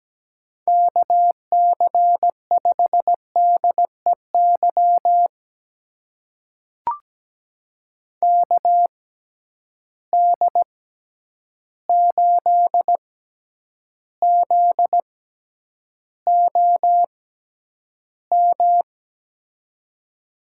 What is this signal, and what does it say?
Morse code, heard as a keyed tone, about 17 wpm, spaced as for 9 wpm, 700 Hz. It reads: KC5DEY KD8ZOM
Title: USA Call Signs - 6 Characters - 6x Character Spacing - 17wpm